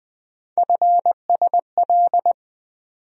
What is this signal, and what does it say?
Morse code, heard as a keyed tone, about 20 wpm, 700 Hz. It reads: FSL
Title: USA Call Sign Postfixes - 3 Letters 20wpm